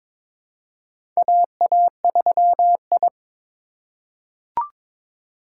Morse code, heard as a keyed tone, about 22 wpm, 700 Hz. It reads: AA3I
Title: USA Call Signs - 4 Characters - 3x Character Spacing - 22wpm